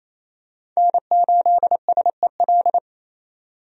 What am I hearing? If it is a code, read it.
Morse code, heard as a keyed tone, about 28 wpm, 700 Hz. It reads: N8SEL
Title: USA Call Signs - 5 Characters - 28wpm